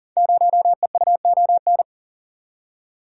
Morse code, heard as a keyed tone, about 42 wpm, 700 Hz. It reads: 0EUON
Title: Code Groups - 5 Letter:Numbers - 40wpm